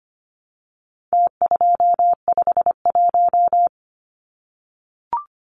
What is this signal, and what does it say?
Morse code, heard as a keyed tone, about 25 wpm, 700 Hz. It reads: T251
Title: Numbers - Contest Style - 4-Digit 25wpm